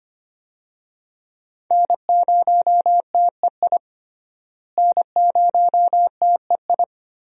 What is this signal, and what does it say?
Morse code, heard as a keyed tone, about 25 wpm, 700 Hz. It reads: N0TEI N0TEI
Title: USA Call Signs - 5 Characters - Repeated 3x - 25wpm